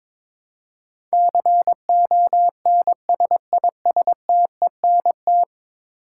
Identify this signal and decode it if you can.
Morse code, heard as a keyed tone, about 21 wpm, 700 Hz. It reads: CONSISTENT
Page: morse-code-ninja-practice-22wpm